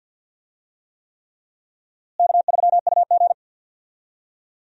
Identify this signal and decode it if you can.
Morse code, heard as a keyed tone, about 46 wpm, 700 Hz. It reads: K3UG
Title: USA Call Signs - 4 Characters - Repeated 3x - 50wpm